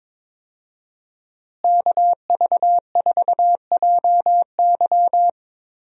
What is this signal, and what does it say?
Morse code, heard as a keyed tone, about 22 wpm, 700 Hz. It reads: KV4JY
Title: USA Call Signs - 5 Characters - Repeated 3x - 22wpm